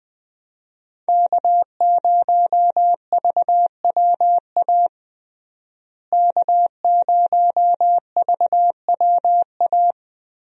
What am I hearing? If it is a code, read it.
Morse code, heard as a keyed tone, about 21 wpm, 700 Hz. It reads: K0VWA K0VWA
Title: USA Call Signs - 5 Characters - Repeated 3x - 20wpm